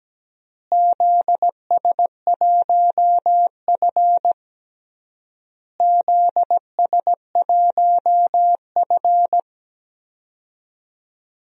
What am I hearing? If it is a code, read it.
Morse code, heard as a keyed tone, about 17 wpm, 700 Hz. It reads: ZS1F ZS1F
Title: DXCC-100 Call Signs - 4 Characters - Repeated 3x - 17wpm